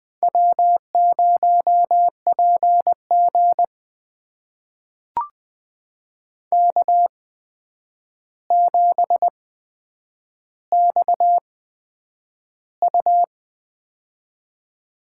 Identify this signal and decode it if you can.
Morse code, heard as a keyed tone, about 20 wpm, spaced as for 9 wpm, 700 Hz. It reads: W0PG K7XU
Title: USA Call Signs - 4 Characters - 8x Character Spacing - 20wpm